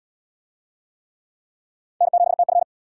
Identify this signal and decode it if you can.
Morse code, heard as a keyed tone, about 69 wpm, 700 Hz. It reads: N7TF